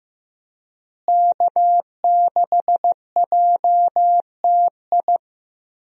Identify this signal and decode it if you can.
Morse code, heard as a keyed tone, about 15 wpm, 700 Hz. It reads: K6JTI
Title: USA Call Signs - 5 Characters - Repeated 3x - 15wpm